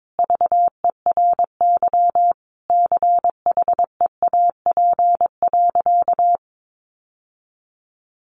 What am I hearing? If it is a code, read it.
Morse code, heard as a keyed tone, about 22 wpm, 700 Hz. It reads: VERY CHEAP.
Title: Sentences from Top 1200 Words 22wpm